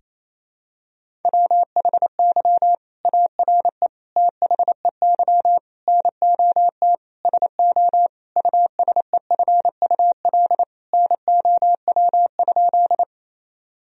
Morse code, heard as a keyed tone, about 28 wpm, 700 Hz. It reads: WHY ARE THEY NOT SO USEFUL NOW?
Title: Sentences from Top 1000 Words - Review Part 4 - 28wpm